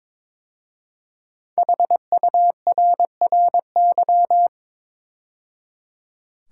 Morse code, heard as a keyed tone, about 22 wpm, 700 Hz. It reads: HURRY
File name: 5 Letter Words - First 3 Letters Given - 22wpm